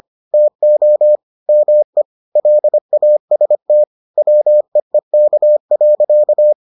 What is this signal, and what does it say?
Morse code, heard as a keyed tone, about 25 wpm, 600 Hz. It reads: TO ME LAST WEEK.